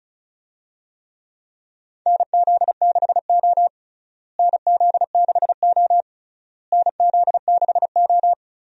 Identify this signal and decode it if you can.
Morse code, heard as a keyed tone, about 34 wpm, 700 Hz. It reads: NZ6O NZ6O NZ6O
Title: USA Call Signs - 4 Characters - Repeated 3x - 35wpm